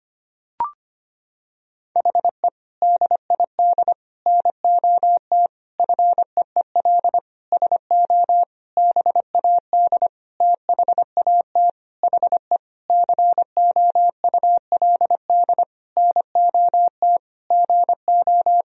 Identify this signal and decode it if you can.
Morse code, heard as a keyed tone, about 25 wpm, 700 Hz. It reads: HE DID NOT FEEL SO BAD THAT HE COULD NOT GO
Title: Sentences from Top 1000 Words - Review Part 4 - 25wpm